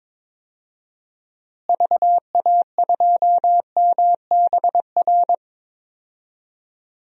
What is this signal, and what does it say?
Morse code, heard as a keyed tone, about 22 wpm, 700 Hz. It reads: VA2MBR
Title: DXCC-100 Call Signs - 6 Characters - Repeated 3x - 22wpm